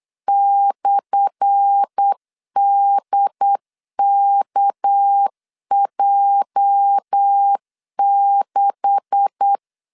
Morse code, heard as a keyed tone, about 8 wpm, 800 Hz. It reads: /DKJ6
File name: LEZIONI / 10 SIGNS / 10A SLASH 8wpm 3452